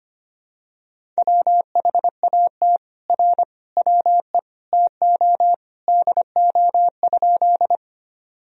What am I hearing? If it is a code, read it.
Morse code, heard as a keyed tone, about 25 wpm, 700 Hz. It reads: WHAT R WE TO DO?
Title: Sentences from Top 100 Words and Abbreviations 25wpm